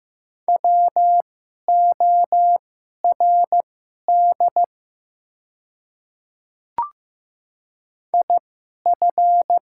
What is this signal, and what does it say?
Morse code, heard as a keyed tone, about 15 wpm, spaced as for 11 wpm, 700 Hz. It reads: WORD IF
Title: 2x Character Spacing - Top 100 Words - 15wpm